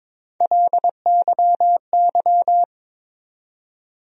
Morse code, heard as a keyed tone, about 22 wpm, 700 Hz. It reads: LYY
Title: USA Call Sign Postfixes - 3 Letters 22wpm